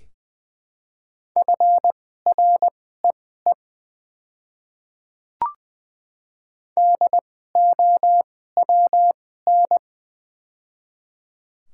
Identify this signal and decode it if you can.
Morse code, heard as a keyed tone, about 20 wpm, spaced as for 14 wpm, 700 Hz. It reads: FREE DOWN
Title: 2x Character Spacing - Top 500 Words - 20wpm